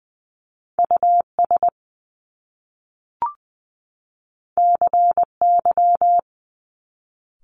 Morse code, heard as a keyed tone, about 20 wpm, 700 Hz. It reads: US CY
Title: Top Two Letter Combinations in Words 20wpm